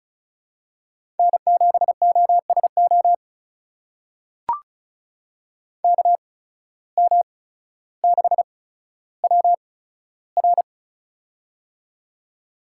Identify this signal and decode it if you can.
Morse code, heard as a keyed tone, about 36 wpm, spaced as for 15 wpm, 700 Hz. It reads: N7OSO KM6WR
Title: USA Call Signs - 5 Characters - 8x Character Spacing - 35wpm